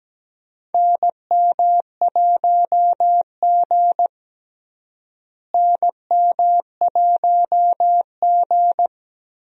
Morse code, heard as a keyed tone, about 17 wpm, 700 Hz. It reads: NM1G NM1G
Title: USA Call Signs - 4 Characters - Repeated 2x - 17wpm